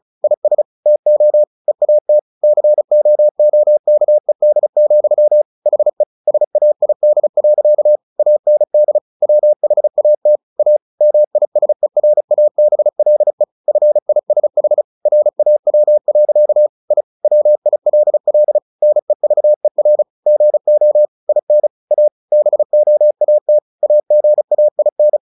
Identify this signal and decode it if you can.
Morse code, heard as a keyed tone, about 34 wpm, 600 Hz. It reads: IS TO EAT COOKED, HE SAID. AND WHAT A MISERABLE FISH RAW. I WILL NEVER GO IN A BOAT AGAIN
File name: the old man and the sea 2 35wpm